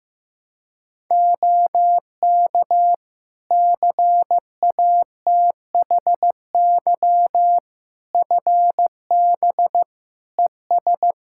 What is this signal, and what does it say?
Morse code, heard as a keyed tone, about 16 wpm, 700 Hz. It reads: OK CATHY FB ES
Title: Practice QSO - Exchange 2 - Info - 15wpm